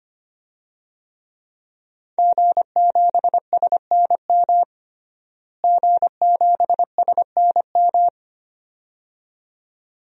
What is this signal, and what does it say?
Morse code, heard as a keyed tone, about 25 wpm, 700 Hz. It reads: G7SNM G7SNM
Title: DXCC-100 Call Signs - 5 Characters - Repeated 2x - 25wpm